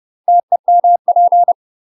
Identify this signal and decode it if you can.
Morse code, heard as a keyed tone, about 30 wpm, 700 Hz. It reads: TEMP
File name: UK MASTER QSOs TRAINING / 30WPM / UK MASTER QSOs 0011